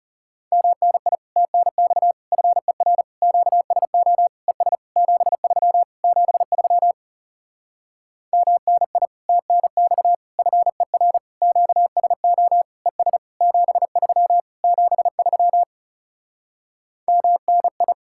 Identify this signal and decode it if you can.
Morse code, heard as a keyed tone, about 34 wpm, 700 Hz. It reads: MNI TNX FER QSO ES 73 73 MNI TNX FER QSO ES 73 73 MNI
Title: Speed-Racing Common QSO Phrases - 30wpm